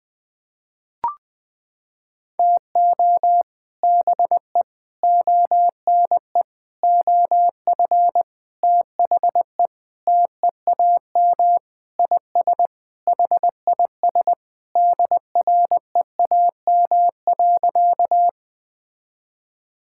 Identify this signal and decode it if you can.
Morse code, heard as a keyed tone, about 20 wpm, 700 Hz. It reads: TO BE ONE OF THE TEAM IS HIS DREAM.
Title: Sentences with 4-Word Phrases - 20wpm